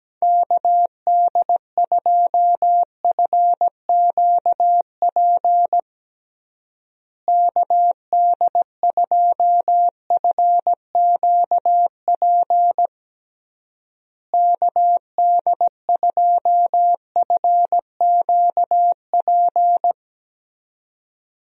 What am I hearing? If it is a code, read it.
Morse code, heard as a keyed tone, about 17 wpm, 700 Hz. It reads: KD2FQP KD2FQP KD2FQP
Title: USA Call Signs - 6 Characters - Repeated 3x - 17wpm